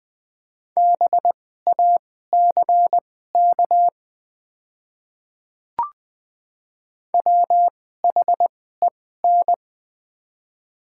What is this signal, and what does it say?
Morse code, heard as a keyed tone, about 20 wpm, spaced as for 14 wpm, 700 Hz. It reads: BACK WHEN
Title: 2x Character Spacing - Top 300 Words - 20wpm